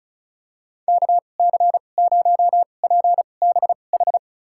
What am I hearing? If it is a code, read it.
Morse code, heard as a keyed tone, about 34 wpm, spaced as for 25 wpm, 700 Hz. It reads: KC0PBH
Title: USA Call Signs - 2x Character Spacing - 35wpm